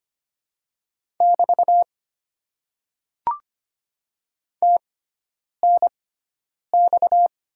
Morse code, heard as a keyed tone, about 25 wpm, spaced as for 9 wpm, 690 Hz. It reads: <BT> TNX
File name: Top 90 QSO Elements - 6x Character Spacing - 25wpm